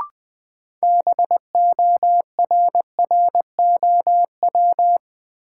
Morse code, heard as a keyed tone, about 20 wpm, 700 Hz. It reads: BORROW
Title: Sentences from Top 1500 Words - New Word Drill Alt - 20wpm